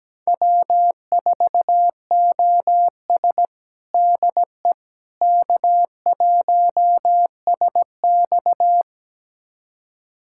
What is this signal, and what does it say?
Morse code, heard as a keyed tone, about 17 wpm, 700 Hz. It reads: W4OS DE K1SX
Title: USA Callsign DE Callsign - 4 Characters - 17wpm